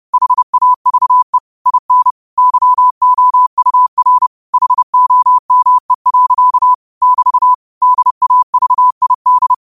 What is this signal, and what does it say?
Morse code, heard as a keyed tone, about 30 wpm, 1000 Hz. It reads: HAVE IN YOUR HOME. <BT> DAVID
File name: qotdfiles / 30wpm